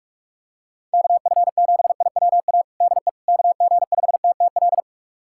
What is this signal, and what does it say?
Morse code, heard as a keyed tone, about 46 wpm, 700 Hz. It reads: KF7IWA DE KG5TTL
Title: USA Callsign DE Callsign 45wpm